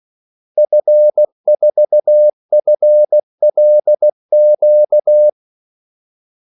Morse code, heard as a keyed tone, about 16 wpm, 600 Hz. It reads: F4FLQ
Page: Kaldesignaler 16 wpm